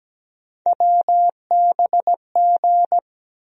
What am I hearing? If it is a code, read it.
Morse code, heard as a keyed tone, about 17 wpm, 700 Hz. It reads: WBG
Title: USA Call Sign Postfixes - 3 Letters 17wpm